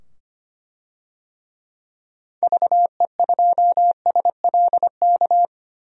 Morse code, heard as a keyed tone, about 25 wpm, 700 Hz. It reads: VE2SLK
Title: DXCC-100 Call Signs - 6 Characters - 25wpm